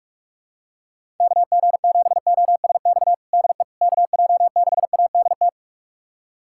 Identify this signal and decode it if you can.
Morse code, heard as a keyed tone, about 47 wpm, 700 Hz. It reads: KG7OSX DE KJ6ADT
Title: USA Callsign DE Callsign 45wpm